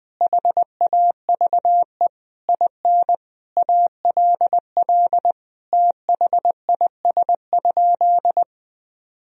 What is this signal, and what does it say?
Morse code, heard as a keyed tone, about 21 wpm, 700 Hz. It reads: HAVE IN ALL THIS?
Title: Sentences from Top 1000 Words - Review Part 2 - 20wpm